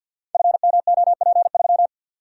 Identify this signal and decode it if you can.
Morse code, heard as a keyed tone, about 50 wpm, 700 Hz. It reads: FMOP3
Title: Code Groups - 5 Letter:Numbers - 50wpm